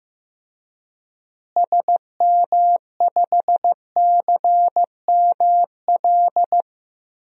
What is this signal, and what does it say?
Morse code, heard as a keyed tone, about 16 wpm, 700 Hz. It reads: SM5CML
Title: DXCC-100 Call Signs - 6 Characters - Repeated 3x - 15wpm